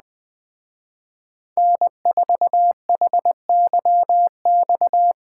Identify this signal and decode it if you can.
Morse code, heard as a keyed tone, about 20 wpm, 700 Hz. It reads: N4HYX